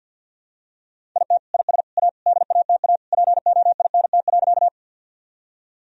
Morse code, heard as a keyed tone, about 46 wpm, 700 Hz. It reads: IT IS A DATA POINT.
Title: Sentences from Top 1000 Words - Review Part 3 - 50wpm